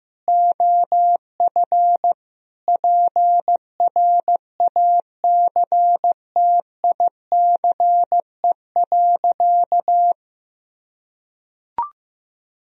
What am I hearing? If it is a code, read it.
Morse code, heard as a keyed tone, about 15 wpm, 700 Hz. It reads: OF PRACTICE.
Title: Sentences from Top 1000 Words - Review Part 4 - 15wpm